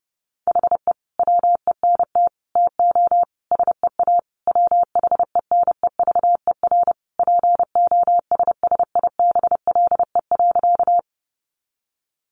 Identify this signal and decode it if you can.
Morse code, heard as a keyed tone, about 30 wpm, 700 Hz. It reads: HE WENT TO SEA WHENEVER POSSIBLE.
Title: Sentences from Top 1200 Words 30wpm